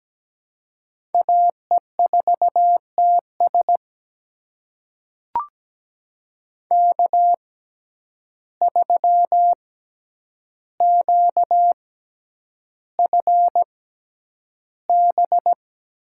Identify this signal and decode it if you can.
Morse code, heard as a keyed tone, about 17 wpm, spaced as for 9 wpm, 700 Hz. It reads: AE4TS K3QFB
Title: USA Call Signs - 5 Characters - 6x Character Spacing - 17wpm